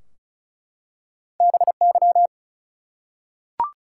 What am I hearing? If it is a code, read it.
Morse code, heard as a keyed tone, about 35 wpm, 700 Hz. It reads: BY
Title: Two Letter Words - 35wpm